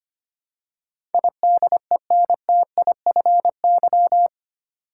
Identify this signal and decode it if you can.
Morse code, heard as a keyed tone, about 25 wpm, 700 Hz. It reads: IDENTIFY
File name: 8 Letter Words - First 3 Letters Given - 25wpm